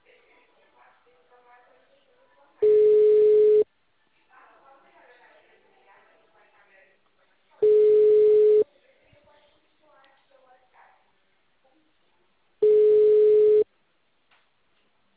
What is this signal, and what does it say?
Ringback tone (Europe, 425 Hz), with speech.